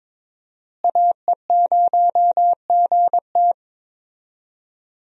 Morse code, heard as a keyed tone, about 21 wpm, 700 Hz. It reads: AE0GT
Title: USA Call Signs - 5 Characters - Repeated 2x - 22wpm